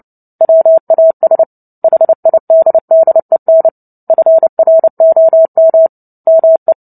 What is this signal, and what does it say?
Morse code, heard as a keyed tone, about 30 wpm, 650 Hz. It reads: WAS HIDDEN FROM ME
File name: Book Two / 03-Chapter 3 - 29 WPM